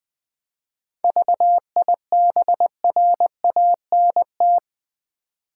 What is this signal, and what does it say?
Morse code, heard as a keyed tone, about 20 wpm, 700 Hz. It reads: VIBRANT
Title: 7 Letter Words - First 3 Letters Given - 20wpm